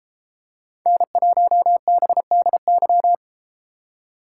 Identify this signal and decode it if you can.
Morse code, heard as a keyed tone, about 31 wpm, 700 Hz. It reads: N1BDY